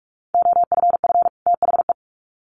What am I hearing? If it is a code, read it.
Morse code, heard as a keyed tone, about 44 wpm, 700 Hz. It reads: OFF THE
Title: Sets of 2 Words 45wpm